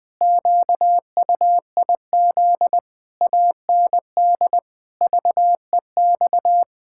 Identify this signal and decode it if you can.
Morse code, heard as a keyed tone, about 20 wpm, 700 Hz. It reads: QUIZ AND VEX
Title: Pangram Challenge v2 - 20wpm